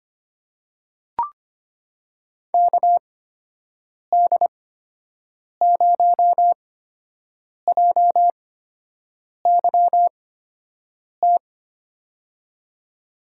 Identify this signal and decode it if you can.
Morse code, heard as a keyed tone, about 25 wpm, spaced as for 7 wpm, 700 Hz. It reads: KD0JYT
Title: USA Call Signs - 6 Characters - 8x Character Spacing - 25wpm